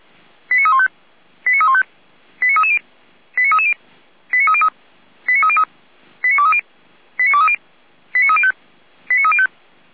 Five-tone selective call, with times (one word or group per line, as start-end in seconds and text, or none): 0.5-0.9 89315
1.5-1.8 89315
2.4-2.8 89330
3.4-3.7 89330
4.3-4.7 89303
5.3-5.6 89303
6.2-6.6 89230
7.2-7.6 89230
8.1-8.5 89305
9.1-9.5 89305